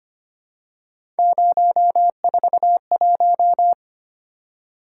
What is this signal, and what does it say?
Morse code, heard as a keyed tone, about 25 wpm, 700 Hz. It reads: 041
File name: Three Letter-Number Combinations - 25wpm